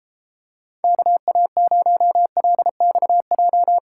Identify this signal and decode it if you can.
Morse code, heard as a keyed tone, about 30 wpm, 700 Hz. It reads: KA0LXJ